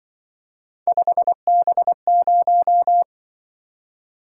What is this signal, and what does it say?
Morse code, heard as a keyed tone, about 25 wpm, 700 Hz. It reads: 5B0